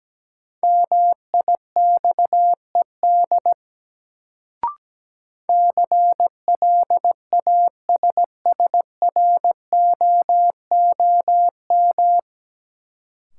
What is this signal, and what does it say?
Morse code, heard as a keyed tone, about 17 wpm, 700 Hz. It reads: MIXED CLASSROOM
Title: Sentences from Top 1500 Words - New Word Drill Alt - 17wpm